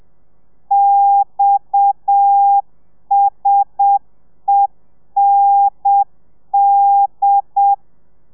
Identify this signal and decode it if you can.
Morse code, heard as a keyed tone, about 7 wpm, 800 Hz. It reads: XSEND
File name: telegrafia / i2tfj / Lez7 30